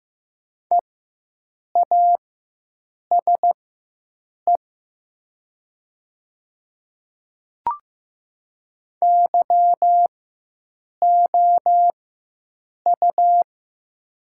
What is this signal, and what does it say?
Morse code, heard as a keyed tone, about 15 wpm, spaced as for 7 wpm, 700 Hz. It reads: EASE YOU
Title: 4x Character Spacing - Top 300 Words - 15wpm